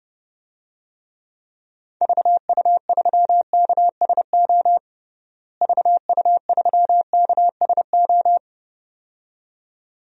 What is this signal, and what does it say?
Morse code, heard as a keyed tone, about 30 wpm, 700 Hz. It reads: VU3KSO VU3KSO
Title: DXCC-100 Call Signs - 6 Characters - Repeated 2x - 30wpm